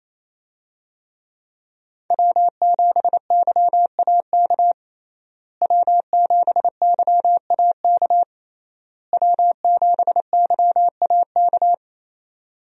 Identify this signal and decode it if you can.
Morse code, heard as a keyed tone, about 28 wpm, 700 Hz. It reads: W7YAK W7YAK W7YAK
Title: USA Call Signs - 5 Characters - Repeated 3x - 28wpm